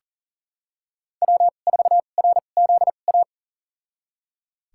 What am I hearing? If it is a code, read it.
Morse code, heard as a keyed tone, about 40 wpm, spaced as for 29 wpm, 700 Hz. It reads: W4RZA
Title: USA Call Signs - 5 Characters - 2x Character Spacing - 40wpm